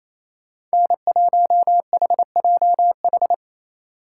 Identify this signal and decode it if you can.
Morse code, heard as a keyed tone, about 33 wpm, 700 Hz. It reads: N1HJH
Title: Speed Racing - USA 1x3 Call Signs 28wpm